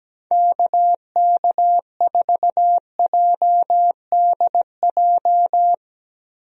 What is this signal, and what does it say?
Morse code, heard as a keyed tone, about 17 wpm, 700 Hz. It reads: KK4JDJ
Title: USA Call Signs - 6 Characters - 17wpm